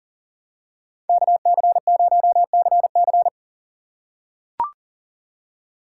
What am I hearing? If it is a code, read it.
Morse code, heard as a keyed tone, about 37 wpm, 700 Hz. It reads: KC0CC
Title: USA Call Signs - 5 Characters - 2x Character Spacing - 40wpm